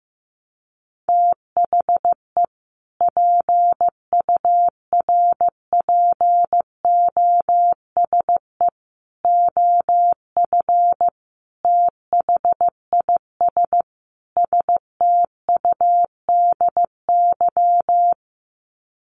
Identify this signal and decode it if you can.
Morse code, heard as a keyed tone, about 15 wpm, 700 Hz. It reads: THE PURPOSE OF THIS STUDY
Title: Sets of 5 Words 15wpm